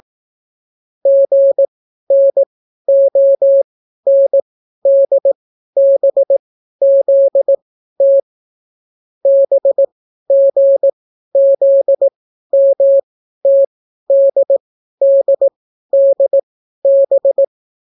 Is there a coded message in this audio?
Yes: Morse code from 1.1 to 17.4 s, heard as a keyed tone, about 18 wpm, spaced as for 12 wpm, 550 Hz. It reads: GNONDBZT BGZMTDDDB